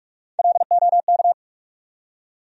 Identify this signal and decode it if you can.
Morse code, heard as a keyed tone, about 46 wpm, 700 Hz. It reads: LOK
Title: USA Call Sign Postfixes - 3 Letters 45wpm